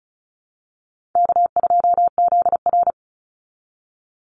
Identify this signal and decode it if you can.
Morse code, heard as a keyed tone, about 34 wpm, 700 Hz. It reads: K2ZR
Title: Easy Call Signs Repeated 2x - 35wpm